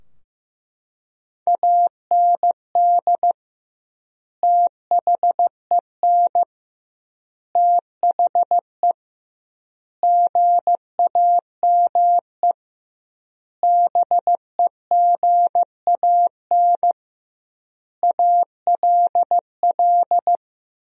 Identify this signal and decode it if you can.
Morse code, heard as a keyed tone, about 15 wpm, 700 Hz. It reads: AND THEN THE GAME BEGAN ALL
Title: 2x Word-Spacing - Sentences from Top 500 Words - 15wpm